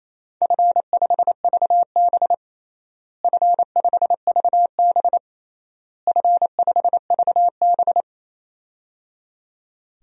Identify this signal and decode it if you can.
Morse code, heard as a keyed tone, about 28 wpm, 700 Hz. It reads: F5VB F5VB F5VB
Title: DXCC-100 Call Signs - 4 Characters - Repeated 3x - 28wpm